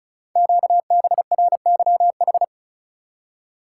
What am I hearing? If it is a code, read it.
Morse code, heard as a keyed tone, about 36 wpm, 690 Hz. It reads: QBRYH
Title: Code Groups - 5 Letters - 35wpm